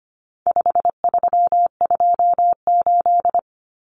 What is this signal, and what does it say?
Morse code, heard as a keyed tone, about 25 wpm, 700 Hz. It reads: 5328
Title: Numbers - 4-Digit 25wpm